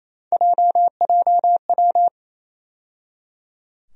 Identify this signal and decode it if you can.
Morse code, heard as a keyed tone, about 28 wpm, 700 Hz. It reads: JJW
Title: USA Call Sign Postfixes - 3 Letters 28wpm